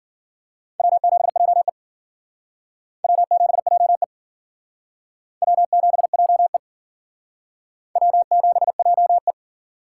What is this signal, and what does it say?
Morse code, heard as a keyed tone, about 51 wpm, 700 Hz. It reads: W7JE W7JE W7JE W7JE